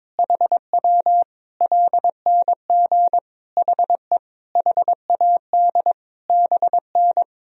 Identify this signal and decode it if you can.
Morse code, heard as a keyed tone, about 22 wpm, 700 Hz. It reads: HW LNG HE HAD BN
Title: Sentences from Top 300 Words and Abbreviations 22wpm